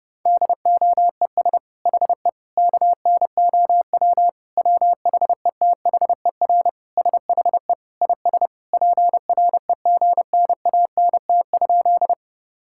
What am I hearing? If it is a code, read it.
Morse code, heard as a keyed tone, about 32 wpm, 700 Hz. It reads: DOES HE KNOW WHETHER SHE IS PREGNANT?
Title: Sentences from Top 1000 Words - Review Part 6 - 30wpm